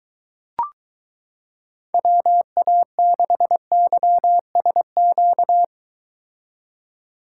Morse code, heard as a keyed tone, about 23 wpm, 700 Hz. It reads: WA6YSQ